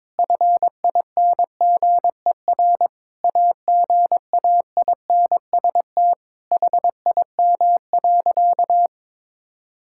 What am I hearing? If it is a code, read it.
Morse code, heard as a keyed tone, about 22 wpm, 700 Hz. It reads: FINGER AGAINST HIM.
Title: Sentences from Top 1600 Words 22wpm